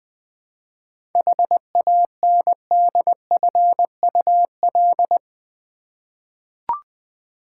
Morse code, heard as a keyed tone, about 20 wpm, 700 Hz. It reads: HANDFUL
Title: 7 Letter Words - First 3 Letters Given - 20wpm